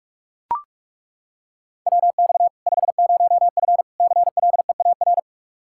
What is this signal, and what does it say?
Morse code, heard as a keyed tone, about 41 wpm, 700 Hz. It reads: WX 50F CLEAR
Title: K1USN SST QSO Practice 45wpm